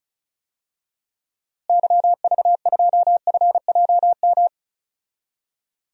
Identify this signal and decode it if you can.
Morse code, heard as a keyed tone, about 34 wpm, 700 Hz. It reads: YV2FJM
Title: DXCC-100 Call Signs - 6 Characters - Repeated 2x - 35wpm